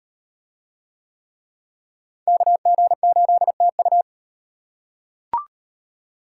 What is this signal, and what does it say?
Morse code, heard as a keyed tone, about 38 wpm, 700 Hz. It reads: KG8TU